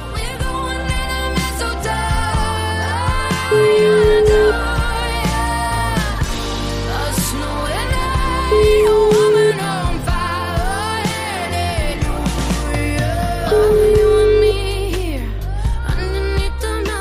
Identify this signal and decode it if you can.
Ringback tone (Europe, 425 Hz), with noise.